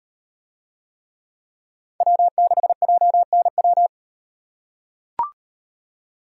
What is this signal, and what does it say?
Morse code, heard as a keyed tone, about 38 wpm, 700 Hz. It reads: W6JNW